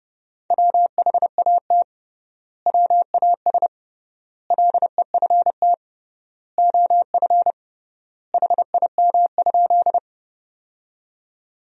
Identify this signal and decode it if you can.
Morse code, heard as a keyed tone, about 30 wpm, 700 Hz. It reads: WHAT WAS LEFT OF HIM?
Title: 3x Word-Spacing - Sentences from Top 300 Words - 30wpm